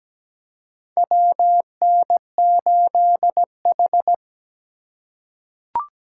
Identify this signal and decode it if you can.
Morse code, heard as a keyed tone, about 17 wpm, 700 Hz. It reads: WN8H
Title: USA Call Signs - 4 Characters - 4x Character Spacing - 17wpm